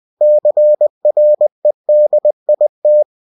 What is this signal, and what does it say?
Morse code, heard as a keyed tone, about 20 wpm, 600 Hz. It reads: CREDIT
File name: cw-books / winnie-the-pooh / winnie-20wpm-page-0122